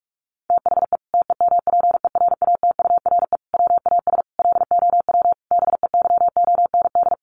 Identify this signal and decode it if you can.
Morse code, heard as a keyed tone, about 44 wpm, 700 Hz. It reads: THE TEMPERATURE WAS LOW BEYOND
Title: Sentences from Top 1500 Words 45wpm